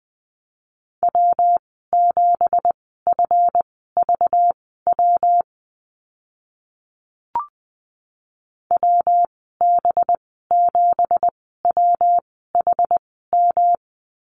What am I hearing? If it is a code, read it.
Morse code, heard as a keyed tone, about 20 wpm, spaced as for 14 wpm, 700 Hz. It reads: W7FVW WB7WHM
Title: 193 Lesson 31g - Letter v - Call Signs 20wpm